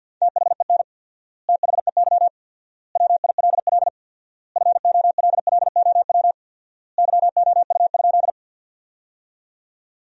Morse code, heard as a keyed tone, about 50 wpm, 700 Hz. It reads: THEN THEY WILL FOLLOW YOU?
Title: 4x Word-Spacing - Sentences from Top 200 Words - 50wpm